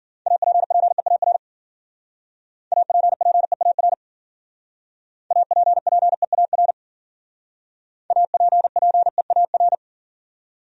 Morse code, heard as a keyed tone, about 51 wpm, 700 Hz. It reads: APPEAR APPEAR APPEAR APPEAR